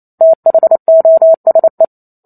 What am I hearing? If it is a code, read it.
Morse code, heard as a keyed tone, about 28 wpm, 650 Hz. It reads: THOSE